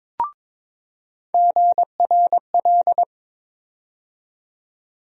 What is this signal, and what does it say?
Morse code, heard as a keyed tone, about 22 wpm, 700 Hz. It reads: GRL
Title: USA Call Sign Postfixes - 3 Letters 22wpm